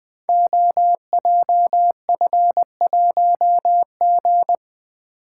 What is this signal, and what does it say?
Morse code, heard as a keyed tone, about 20 wpm, 700 Hz. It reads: OJF1G
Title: Code Groups - 5 Letter:Numbers - 20wpm